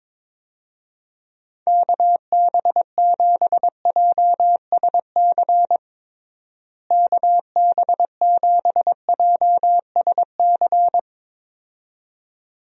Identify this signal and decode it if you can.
Morse code, heard as a keyed tone, about 21 wpm, 700 Hz. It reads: KB7JSC KB7JSC
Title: USA Call Signs - 6 Characters - Repeated 2x - 22wpm